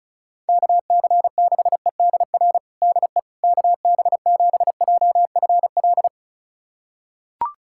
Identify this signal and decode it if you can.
Morse code, heard as a keyed tone, about 36 wpm, 700 Hz. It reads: KC6EDR DE KB7JFL
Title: USA Callsign DE Callsign 35wpm